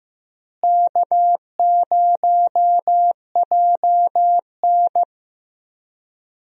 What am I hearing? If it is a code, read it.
Morse code, heard as a keyed tone, about 15 wpm, 700 Hz. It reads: K0JN
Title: USA Call Signs - 4 Characters - Repeated 3x - 15wpm